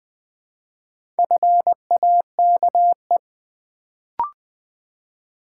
Morse code, heard as a keyed tone, about 20 wpm, 700 Hz. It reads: FAKE
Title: Build A Word - 2 Letters At A Time - 4-Letter Words - 20wpm